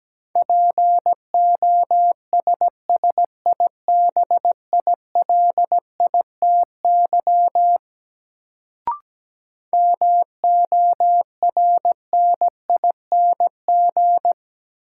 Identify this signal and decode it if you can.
Morse code, heard as a keyed tone, about 17 wpm, 700 Hz. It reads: POSSIBILITY MORNING
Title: Sets of 3 Words - New Word Drill - 17wpm